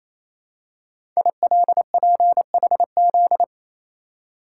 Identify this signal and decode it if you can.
Morse code, heard as a keyed tone, about 27 wpm, 700 Hz. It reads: ILPHZ
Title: Code Groups - 5 Letter:Numbers - 28wpm